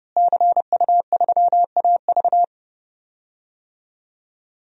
Morse code, heard as a keyed tone, about 30 wpm, 700 Hz. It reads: CU3AV
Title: DXCC-100 Call Signs - 5 Characters - Repeated 2x - 30wpm